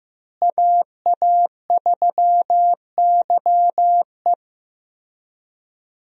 Morse code, heard as a keyed tone, about 15 wpm, 700 Hz. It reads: AA3YE
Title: USA Call Signs - 5 Characters - Repeated 3x - 15wpm